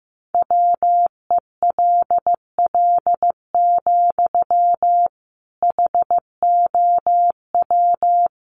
Morse code, heard as a keyed tone, about 15 wpm, 700 Hz. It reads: WELL, HOW
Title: Sentences from Top 1100 Words 15wpm